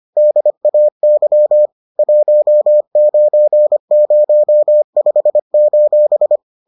Morse code, heard as a keyed tone, about 25 wpm, 600 Hz. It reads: DAY 1905: